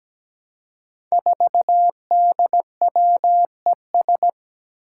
Morse code, heard as a keyed tone, about 17 wpm, 700 Hz. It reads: 4DWES
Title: Five Challenging Letter-Number Combinations 17wpm